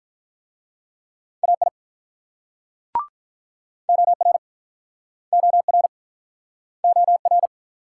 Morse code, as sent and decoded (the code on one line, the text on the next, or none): .- .. / --- .-. / --- .-. / --- .-.
AI OR OR OR